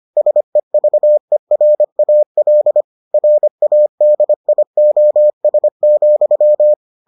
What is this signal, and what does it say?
Morse code, heard as a keyed tone, about 25 wpm, 600 Hz. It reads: SEVERAL RADIOS,